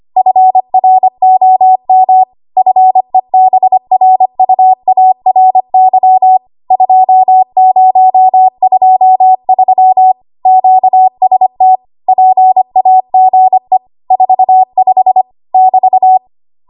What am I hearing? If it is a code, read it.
Morse code, heard as a keyed tone, about 25 wpm, 750 Hz. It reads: FROM FEBRUARY 2023 QST PAGE 45 <BT>
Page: 25 WPM Code Archive